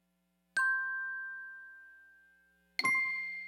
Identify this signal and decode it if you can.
There is no code.